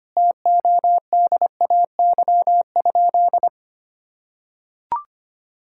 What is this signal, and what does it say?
Morse code, heard as a keyed tone, about 25 wpm, 700 Hz. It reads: TODAY?
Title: Sentences from Top 1000 Words - Review Part 6 - 25wpm